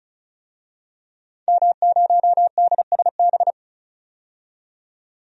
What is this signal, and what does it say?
Morse code, heard as a keyed tone, about 36 wpm, 700 Hz. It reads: M0DSB
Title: DXCC-100 Call Signs - 5 Characters - 35wpm